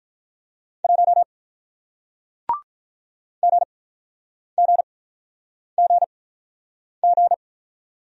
Morse code, heard as a keyed tone, about 48 wpm, spaced as for 9 wpm, 700 Hz. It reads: 1 GGGG